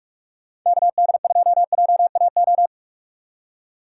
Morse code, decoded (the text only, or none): KD2JAO